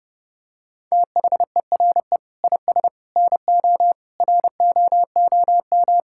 Morse code, heard as a keyed tone, about 31 wpm, 700 Hz. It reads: THERE IS NO ROOM
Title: Sentences from Top 1000 Words - Review Part 5 - 30wpm